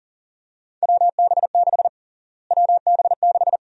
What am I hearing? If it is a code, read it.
Morse code, heard as a keyed tone, about 37 wpm, 700 Hz. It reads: WB6 WB6
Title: USA Call Sign Prefixes - Repeated 2x - 40wpm